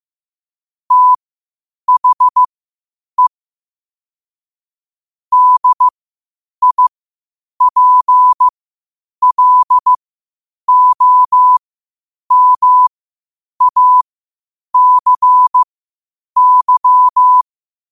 Morse code, heard as a keyed tone, about 15 wpm, spaced as for 8 wpm, 1000 Hz. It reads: THE DIPLOMACY
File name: qotdfiles / 8wpm